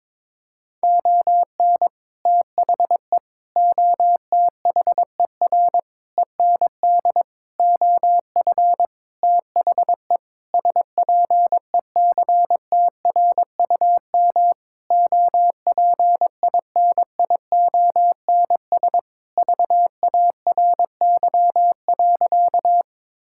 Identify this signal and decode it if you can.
Morse code, heard as a keyed tone, about 22 wpm, 700 Hz. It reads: ON THE OTHER END OF THE SPECTRUM OPINIONS VARY.
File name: Sentences with 7-Word Phrases - 22wpm